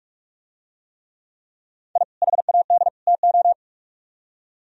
Morse code, heard as a keyed tone, about 42 wpm, 700 Hz. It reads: I HAD TO